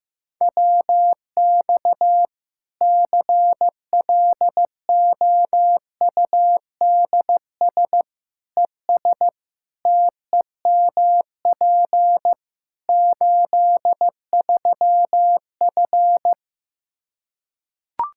Morse code, heard as a keyed tone, about 15 wpm, 700 Hz. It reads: WX CLOUDS ES TEMP 83F
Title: K1USN SST QSO Practice 15wpm